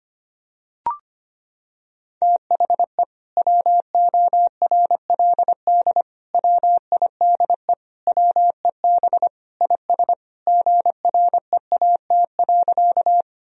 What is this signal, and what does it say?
Morse code, heard as a keyed tone, about 25 wpm, 700 Hz. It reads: THE WORLD WIDE WEB IS GREAT.
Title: Sentences from Top 1000 Words - Review Part 4 - 25wpm